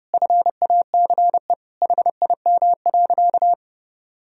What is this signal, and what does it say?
Morse code, heard as a keyed tone, about 30 wpm, 700 Hz. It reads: FACE HIM.
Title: Sentences from Top 1000 Words - Review Part 2 - 30wpm